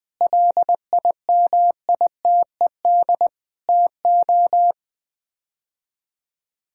Morse code, heard as a keyed tone, about 20 wpm, 700 Hz. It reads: LIMITED TO
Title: Sets of 6 Words - 20wpm